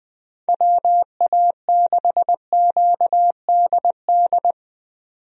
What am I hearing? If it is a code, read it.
Morse code, heard as a keyed tone, about 20 wpm, 700 Hz. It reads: WA6QDD
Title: USA Call Signs - 6 Characters - 20wpm